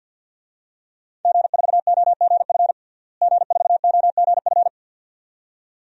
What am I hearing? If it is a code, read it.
Morse code, heard as a keyed tone, about 53 wpm, 700 Hz. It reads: G4OGF G4OGF
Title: DXCC-100 Call Signs - 5 Characters - Repeated 2x - 50wpm